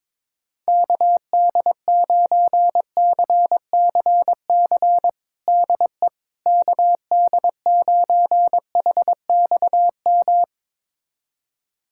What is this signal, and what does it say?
Morse code, heard as a keyed tone, about 22 wpm, 700 Hz. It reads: KD9CCC DE KD9HXM
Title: USA Callsign DE Callsign - 6 Characters - 22wpm